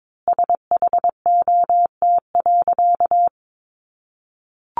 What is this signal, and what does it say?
Morse code, heard as a keyed tone, about 22 wpm, 700 Hz. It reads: SHOT.
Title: A Sound Of Thunder 22wpm